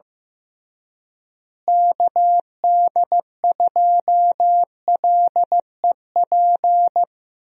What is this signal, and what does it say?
Morse code, heard as a keyed tone, about 15 wpm, 700 Hz. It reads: KD2LEP
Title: USA Call Signs - 6 Characters - Repeated 3x - 15wpm